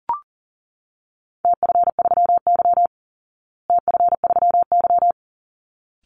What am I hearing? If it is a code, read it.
Morse code, heard as a keyed tone, about 40 wpm, 700 Hz. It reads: TF3Y TF3Y
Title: Call Signs Repeated 3x - 40wpm